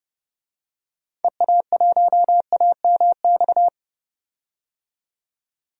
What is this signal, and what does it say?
Morse code, heard as a keyed tone, about 32 wpm, 700 Hz. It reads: EA1AMX
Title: DXCC-100 Call Signs - 6 Characters - 30wpm